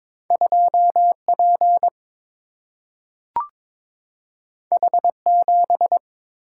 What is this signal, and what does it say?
Morse code, heard as a keyed tone, about 22 wpm, 700 Hz. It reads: 2P H7
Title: Single-Letter Single-Number 22wpm